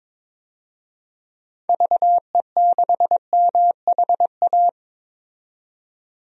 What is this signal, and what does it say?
Morse code, heard as a keyed tone, about 22 wpm, 700 Hz. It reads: VE6MHA
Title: DXCC-100 Call Signs - 6 Characters - 22wpm